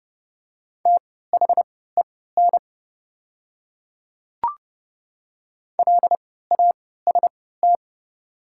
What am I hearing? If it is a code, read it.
Morse code, heard as a keyed tone, about 30 wpm, spaced as for 17 wpm, 690 Hz. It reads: THEN LAST
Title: 3x Character Spacing - Top 300 Words - 30wpm